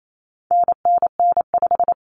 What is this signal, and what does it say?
Morse code, heard as a keyed tone, about 28 wpm, 700 Hz. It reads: NNN5
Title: Common Cut Numbers - 4-Digit 28wpm